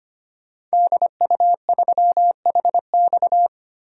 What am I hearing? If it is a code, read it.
Morse code, heard as a keyed tone, about 25 wpm, 700 Hz. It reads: DU3HX
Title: DXCC-100 Call Signs - 5 Characters - Repeated 2x - 25wpm